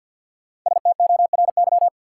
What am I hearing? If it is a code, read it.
Morse code, heard as a keyed tone, about 47 wpm, 700 Hz. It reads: STORY
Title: Sets of 3 Words - New Word Drill - 50wpm